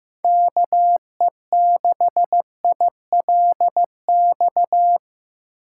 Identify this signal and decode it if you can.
Morse code, heard as a keyed tone, about 15 wpm, 700 Hz. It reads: KE6ILX
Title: USA Call Signs - 6 Characters - 3x Character Spacing - 15wpm